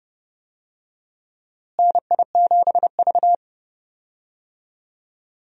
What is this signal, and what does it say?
Morse code, heard as a keyed tone, about 30 wpm, 700 Hz. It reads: NI7V
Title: USA Call Signs - 4 Characters - Repeated 3x - 30wpm